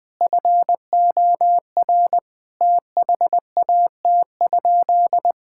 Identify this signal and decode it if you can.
Morse code, heard as a keyed tone, about 21 wpm, 700 Hz. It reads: FOR THAT?
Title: Sentences from Top 1000 Words - Review Part 6 - 20wpm